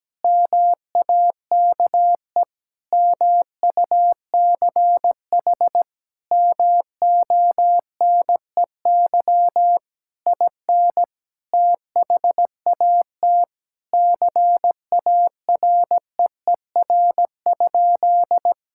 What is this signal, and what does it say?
Morse code, heard as a keyed tone, about 17 wpm, 700 Hz. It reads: MAKE MUCH MONEY IN THAT CAREER?
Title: Sentences from Top 1000 Words - Review Part 6 - 17wpm